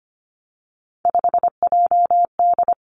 Morse code, heard as a keyed tone, about 25 wpm, 700 Hz. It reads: 5JD
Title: Three Challenging Letter-Number Combinations 25wpm